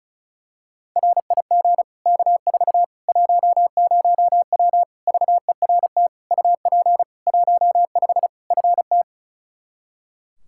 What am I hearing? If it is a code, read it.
Morse code, heard as a keyed tone, about 36 wpm, 700 Hz. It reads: RIG K4 10W VERT UP 15 FT
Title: Practice QSO - Exchange 2 - Info - 35wpm